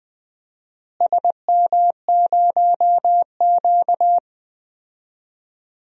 Morse code, heard as a keyed tone, about 21 wpm, 700 Hz. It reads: SM0Q
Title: DXCC-100 Call Signs - 4 Characters - Repeated 2x - 20wpm